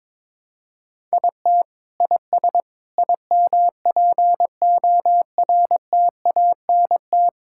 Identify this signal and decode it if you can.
Morse code, heard as a keyed tone, about 21 wpm, 700 Hz. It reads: IT IS IMPORTANT
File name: practice / 22wpm / Sentences with 8-Word Phrases - 22wpm